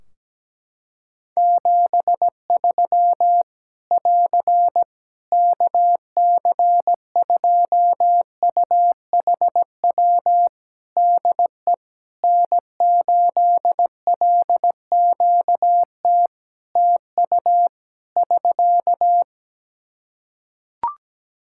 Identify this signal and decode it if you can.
Morse code, heard as a keyed tone, about 17 wpm, 700 Hz. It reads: 73 <AR> KC2UHW DE N8LQT TU <SK>
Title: Practice QSO - Exchange 4 - Ending - 17wpm